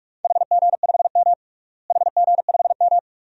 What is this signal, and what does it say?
Morse code, heard as a keyed tone, about 41 wpm, 700 Hz. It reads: HG5M HG5M
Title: DXCC-100 Call Signs - 4 Characters - Repeated 2x - 45wpm